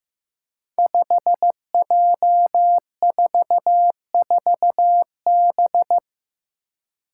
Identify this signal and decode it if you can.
Morse code, heard as a keyed tone, about 15 wpm, 700 Hz. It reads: 5J44B
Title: Five Challenging Letter-Number Combinations 15wpm